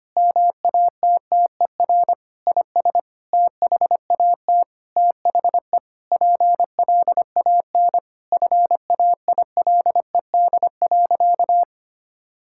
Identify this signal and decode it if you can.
Morse code, heard as a keyed tone, about 25 wpm, 700 Hz. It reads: MATTER IS THAT THE PLAN FAILED.
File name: practice / 25wpm / Sentences with 9-Word Phrases - 25wpm